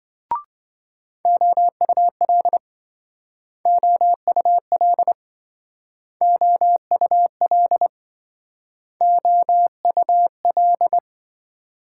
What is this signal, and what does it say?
Morse code, heard as a keyed tone, about 26 wpm, 700 Hz. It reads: OUL OUL OUL OUL